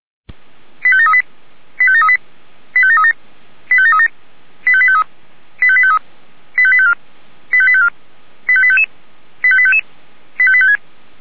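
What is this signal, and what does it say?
Five-tone selective call (ZVEI), heard as quick runs of five tones, tones of about 70 ms. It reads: 85738 85738 85737 85737 85853 85853 85854 85854 85855 85855 85856
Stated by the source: ZVEI five-tone call 85745 received on SDR.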